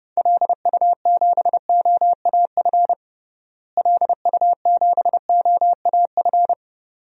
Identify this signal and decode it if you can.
Morse code, heard as a keyed tone, about 30 wpm, 700 Hz. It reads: LU7OAF LU7OAF
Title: DXCC-100 Call Signs - 6 Characters - Repeated 2x - 30wpm